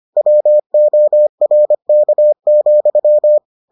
Morse code, heard as a keyed tone, about 25 wpm, 600 Hz. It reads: WORK,